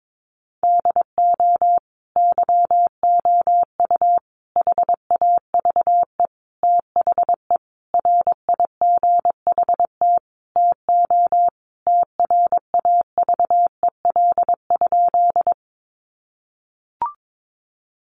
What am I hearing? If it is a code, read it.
Morse code, heard as a keyed tone, about 22 wpm, 700 Hz. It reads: DO YOU HAVE THE RIGHT TO TRAVEL?
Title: Sentences from Top 1400 Words 22wpm